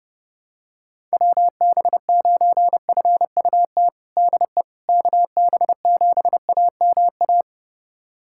Morse code, heard as a keyed tone, about 30 wpm, 700 Hz. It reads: WB9FUT DE KB7AMA
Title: USA Callsign DE Callsign - 6 Characters - 30wpm